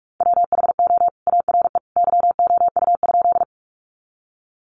Morse code, heard as a keyed tone, about 44 wpm, 700 Hz. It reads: WHO ARE YOU?
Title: Sentences from Top 100 Words 45wpm